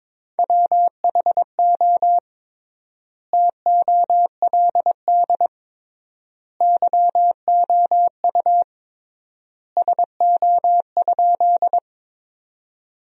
Morse code, heard as a keyed tone, about 22 wpm, 700 Hz. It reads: WHO TOLD YOU SO?
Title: 3x Word-Spacing - Sentences from Top 400 Words - 22wpm